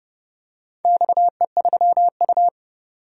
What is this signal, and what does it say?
Morse code, heard as a keyed tone, about 31 wpm, 700 Hz. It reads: XE3U
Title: DXCC-100 Call Signs - 4 Characters - 30wpm